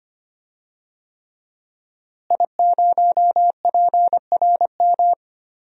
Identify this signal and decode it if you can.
Morse code, heard as a keyed tone, about 25 wpm, 700 Hz. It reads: I0PRM
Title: DXCC-100 Call Signs - 5 Characters - Repeated 3x - 25wpm